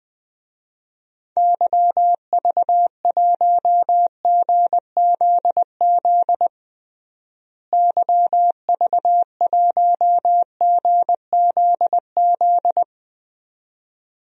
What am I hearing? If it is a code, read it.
Morse code, heard as a keyed tone, about 20 wpm, 700 Hz. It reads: YV1GZZ YV1GZZ
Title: DXCC-100 Call Signs - 6 Characters - Repeated 2x - 20wpm